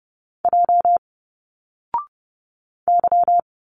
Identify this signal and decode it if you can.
Morse code, heard as a keyed tone, about 30 wpm, 700 Hz. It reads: J Y
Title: Single Letters 30wpm